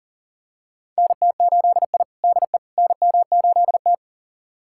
Morse code, heard as a keyed tone, about 40 wpm, 700 Hz. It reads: NT8I DE NM8T
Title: USA Callsign DE Callsign - 4 Characters - 40wpm